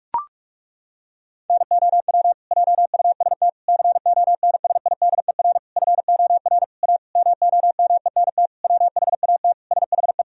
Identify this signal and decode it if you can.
Morse code, heard as a keyed tone, about 41 wpm, 700 Hz. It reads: NOW JUST CONSIDER FOR A MOMENT WHAT SHE
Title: Sentences from Top 1000 Words - Review Part 4 - 45wpm